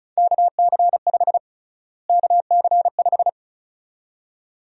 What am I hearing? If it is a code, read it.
Morse code, heard as a keyed tone, about 36 wpm, 700 Hz. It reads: KC5 KC5
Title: USA Call Sign Prefixes - Repeated 2x - 35wpm